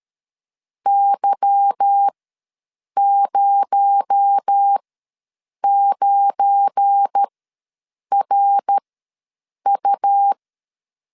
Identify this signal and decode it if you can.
Morse code, heard as a keyed tone, about 13 wpm, spaced as for 7 wpm, 800 Hz. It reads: Y09RU
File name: LEZIONI / 8 YJ45 / 8 YJ ws 12wpm 3342